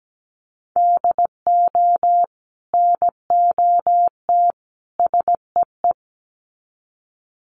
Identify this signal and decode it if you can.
Morse code, heard as a keyed tone, about 17 wpm, 700 Hz. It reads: DO NOT SEE
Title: Sets of 3 Words 17wpm